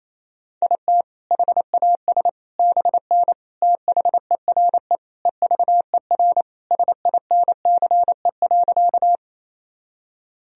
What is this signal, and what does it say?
Morse code, heard as a keyed tone, about 28 wpm, 700 Hz. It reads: IT HAS BN THERE EVER SINCE.
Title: Sentences from Top 400 Words and Abbreviations 28wpm